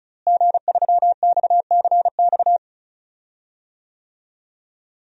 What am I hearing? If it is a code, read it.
Morse code, heard as a keyed tone, about 34 wpm, 700 Hz. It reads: G3XCX
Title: DXCC-100 Call Signs - 5 Characters - Repeated 3x - 35wpm